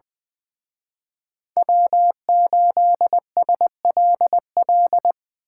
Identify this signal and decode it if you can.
Morse code, heard as a keyed tone, about 24 wpm, 700 Hz. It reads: W8SLL
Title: Speed Racing - USA 1x3 Call Signs 20wpm